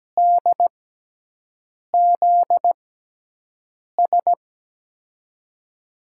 Morse code, heard as a keyed tone, about 17 wpm, spaced as for 6 wpm, 700 Hz. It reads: DZS
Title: USA Call Signs - 6 Characters - 6x Character Spacing - 17wpm